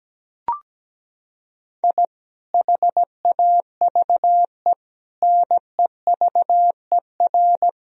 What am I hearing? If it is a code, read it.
Morse code, heard as a keyed tone, about 17 wpm, 700 Hz. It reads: I HAVE NEVER
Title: Sentences from Top 1000 Words - Review Part 3 - 17wpm